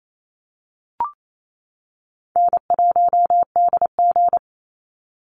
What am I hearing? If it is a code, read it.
Morse code, heard as a keyed tone, about 28 wpm, 700 Hz. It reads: N1DG
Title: Easy Call Signs - 28wpm